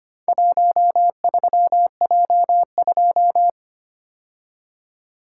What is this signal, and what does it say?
Morse code, heard as a keyed tone, about 25 wpm, 700 Hz. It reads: 13J2
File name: practice / 25wpm / Four Letter-Number Combinations - 25wpm